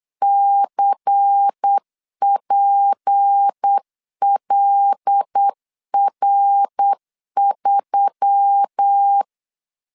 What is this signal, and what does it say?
Morse code, heard as a keyed tone, about 8 wpm, 800 Hz. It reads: CPLR3
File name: LEZIONI / 6 PKX0 / 6 P 8wpm 3025